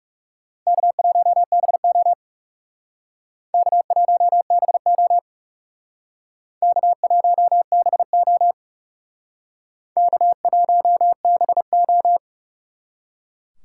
Morse code, heard as a keyed tone, about 38 wpm, 700 Hz. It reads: K1BO K1BO K1BO K1BO
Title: Speed Racing - USA Call Signs 30wpm